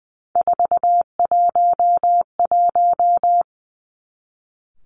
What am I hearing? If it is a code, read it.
Morse code, heard as a keyed tone, about 20 wpm, 700 Hz. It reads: 411
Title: Three Challenging Letter-Number Combinations 20wpm